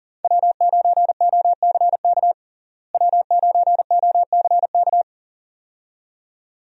Morse code, heard as a keyed tone, about 40 wpm, 700 Hz. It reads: W9OCK W9OCK
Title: USA Call Signs - 5 Characters - Repeated 2x - 40wpm